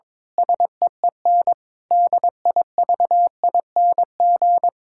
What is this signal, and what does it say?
Morse code, heard as a keyed tone, about 22 wpm, 700 Hz. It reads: SEEN DIVING
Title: Pangram Challenge v2 - 22wpm